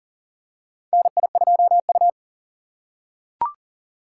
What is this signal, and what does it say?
Morse code, heard as a keyed tone, about 40 wpm, 700 Hz. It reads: NI2U
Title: USA Call Signs - 2x Character Spacing - 40wpm